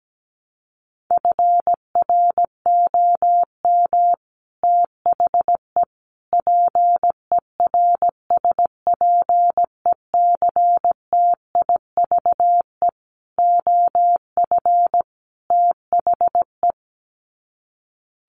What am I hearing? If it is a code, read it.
Morse code, heard as a keyed tone, about 17 wpm, 700 Hz. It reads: FROM THE PERSPECTIVE OF THE
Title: Sets of 5 Words Encore2 17wpm